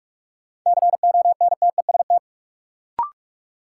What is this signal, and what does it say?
Morse code, heard as a keyed tone, about 46 wpm, 700 Hz. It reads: CONTEST
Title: Sentences from Top 1600 Words - New Word Drill Alt - 45wpm